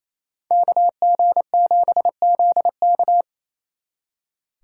Morse code, heard as a keyed tone, about 37 wpm, 700 Hz. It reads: KG7ZK